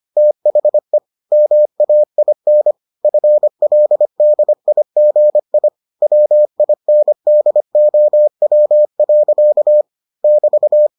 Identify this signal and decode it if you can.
Morse code, heard as a keyed tone, about 25 wpm, 600 Hz. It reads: THE MAIN FLDIGI WINDOW. <BT>